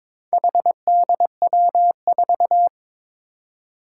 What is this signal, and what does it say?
Morse code, heard as a keyed tone, about 23 wpm, 700 Hz. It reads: HDW4
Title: Four Challenging Letter-Number Combinations 22wpm